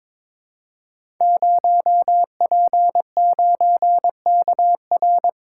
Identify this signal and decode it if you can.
Morse code, heard as a keyed tone, about 22 wpm, 700 Hz. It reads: 0P9KR
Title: Code Groups - 5 Letter:Numbers - 22wpm